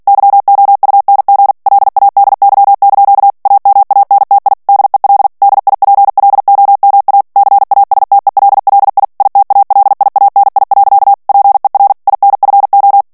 Morse code, heard as a keyed tone, about 48 wpm, 800 Hz. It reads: YOUNG LADY) AMANTI DEL DIPLOMA CASTELLI ITALIANI. PER INFO
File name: morse mp3 / ITALIEN texte clair 1 vitesse 45 mots minute